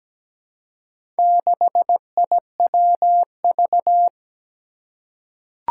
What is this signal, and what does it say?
Morse code, heard as a keyed tone, about 17 wpm, 700 Hz. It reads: 6IWV
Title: Four Challenging Letter-Number Combinations 17wpm